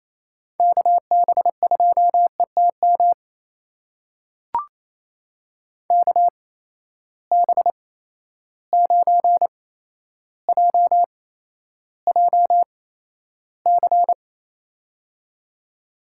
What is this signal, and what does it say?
Morse code, heard as a keyed tone, about 28 wpm, spaced as for 12 wpm, 700 Hz. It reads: KB2ETM KB9JJC
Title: USA Call Signs - 6 Characters - 8x Character Spacing - 28wpm